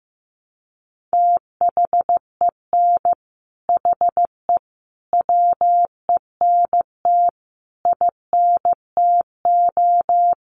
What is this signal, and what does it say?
Morse code, heard as a keyed tone, about 15 wpm, 700 Hz. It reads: THEN HE WENT INTO
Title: Sentences from Top 1500 Words 15wpm